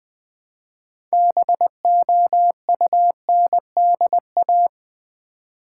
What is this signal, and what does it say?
Morse code, heard as a keyed tone, about 20 wpm, 700 Hz. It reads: BOUNDA
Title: Build A Word - 3 Letters At A Time - 8-Letter Words - 20wpm